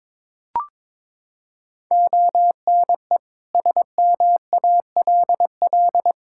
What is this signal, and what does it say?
Morse code, heard as a keyed tone, about 22 wpm, 700 Hz. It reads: ONE SMALL
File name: Sets of 9 Words - 22wpm